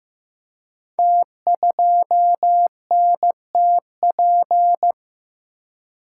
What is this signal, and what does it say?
Morse code, heard as a keyed tone, about 15 wpm, 700 Hz. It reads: T2NTP
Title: Code Groups - 5 Letter:Numbers - 15wpm